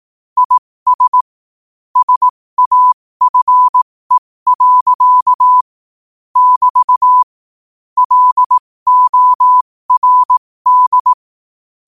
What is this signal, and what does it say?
Morse code, heard as a keyed tone, about 18 wpm, spaced as for 15 wpm, 1000 Hz. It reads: IS SAFE. <BT> LORD
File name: qotdfiles / 15wpm